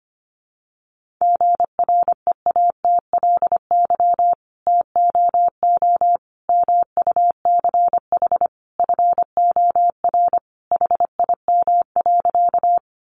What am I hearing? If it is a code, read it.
Morse code, heard as a keyed tone, about 25 wpm, 700 Hz. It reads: GREATLY TOO MUCH FOR HIM.
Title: Sentences from Top 1400 Words 25wpm